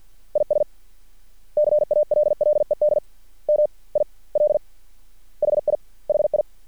Morse code, heard as a keyed tone, about 50 wpm, 600 Hz. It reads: IS CALLED M I L HI HI
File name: archived / CRSnet